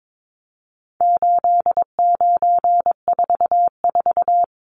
Morse code, heard as a keyed tone, about 22 wpm, 700 Hz. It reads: 8944
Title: Numbers - 4-Digit 22wpm